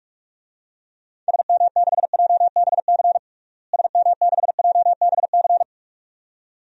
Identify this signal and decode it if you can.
Morse code, heard as a keyed tone, about 47 wpm, 700 Hz. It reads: SM6JBC SM6JBC
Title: DXCC-100 Call Signs - 6 Characters - Repeated 2x - 45wpm